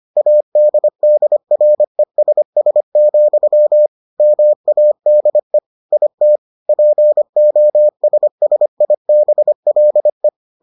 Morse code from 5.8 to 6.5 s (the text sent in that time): IT